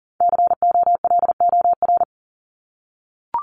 Morse code, heard as a keyed tone, about 40 wpm, 700 Hz. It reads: COLOR
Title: Top 500 Words Repeated 40wpm